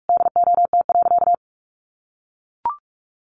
Morse code, heard as a keyed tone, about 44 wpm, 700 Hz. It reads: DOT.
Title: Sentences from Top 1500 Words 45wpm